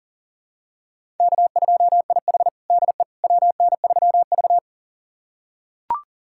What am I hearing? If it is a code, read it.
Morse code, heard as a keyed tone, about 40 wpm, 700 Hz. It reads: K2IH DE WN3V
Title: USA Callsign DE Callsign - 4 Characters - 40wpm